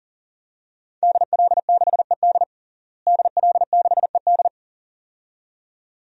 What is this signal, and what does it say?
Morse code, heard as a keyed tone, about 37 wpm, 700 Hz. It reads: DL6ED DL6ED
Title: DXCC-100 Call Signs - 5 Characters - Repeated 2x - 40wpm